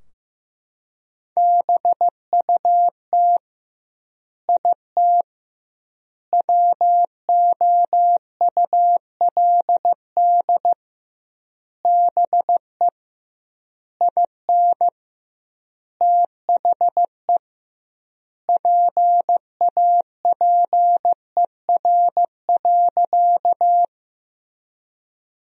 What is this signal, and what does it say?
Morse code, heard as a keyed tone, about 15 wpm, 700 Hz. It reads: BUT IT WOULD BE IN THE PAPER.
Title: 2x Word-Spacing - Sentences from Top 300 Words - 15wpm